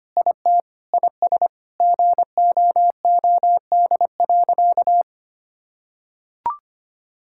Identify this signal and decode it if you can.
Morse code, heard as a keyed tone, about 25 wpm, 700 Hz. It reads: IT IS GOOD.
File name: practice / 25wpm / Sentences with 3-Word Phrases - 25wpm